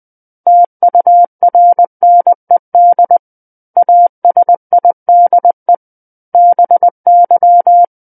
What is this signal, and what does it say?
Morse code, heard as a keyed tone, about 20 wpm, 700 Hz. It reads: TURNED ASIDE BY
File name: 20 WPM